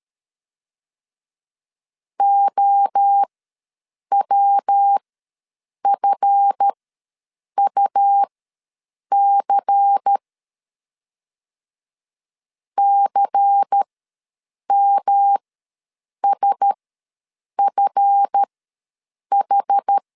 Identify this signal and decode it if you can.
Morse code, heard as a keyed tone, about 13 wpm, spaced as for 7 wpm, 800 Hz. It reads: OWFUC CMSFH
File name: LEZIONI / 7 GF9Z / 7 GF ws 12wpm 3140